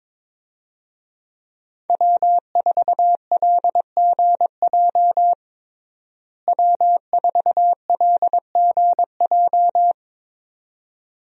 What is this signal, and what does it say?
Morse code, heard as a keyed tone, about 22 wpm, 700 Hz. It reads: W4LGJ W4LGJ
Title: USA Call Signs - 5 Characters - Repeated 2x - 22wpm